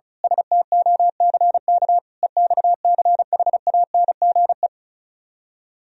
Morse code, heard as a keyed tone, about 32 wpm, 700 Hz. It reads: STOCK EXCHANGE
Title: Speed-Racing - Sets of 5 Words 35wpm